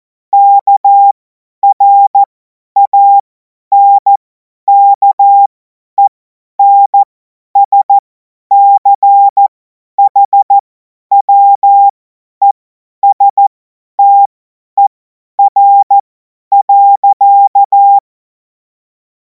Morse code, heard as a keyed tone, about 14 wpm, spaced as for 10 wpm, 800 Hz. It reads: KRANKENSCHWESTER.
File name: Morsekurs / Witze / Witz 04. August 2026 50 F70